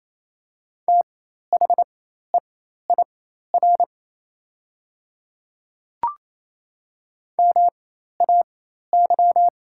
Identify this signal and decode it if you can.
Morse code, heard as a keyed tone, about 28 wpm, spaced as for 13 wpm, 700 Hz. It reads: THEIR MAY
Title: 4x Character Spacing - Top 100 Words - 28wpm